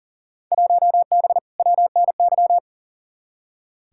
Morse code, heard as a keyed tone, about 40 wpm, 700 Hz. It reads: 1B WNY
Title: Field Day Exchanges Encore - 40wpm